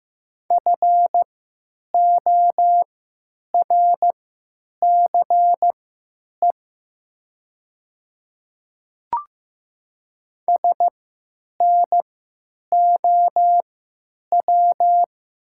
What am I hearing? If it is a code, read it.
Morse code, heard as a keyed tone, about 15 wpm, spaced as for 9 wpm, 700 Hz. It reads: FORCE SNOW
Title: 3x Character Spacing - Top 500 Words - 15wpm